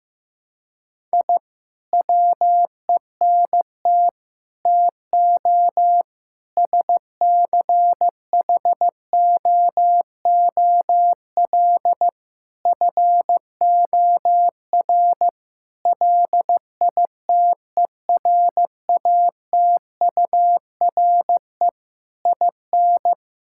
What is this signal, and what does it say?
Morse code, heard as a keyed tone, about 16 wpm, 700 Hz. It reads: I WENT TO SCHOOL FOR LITERATURE IN
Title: Sentences from Top 1000 Words - Review Part 6 - 15wpm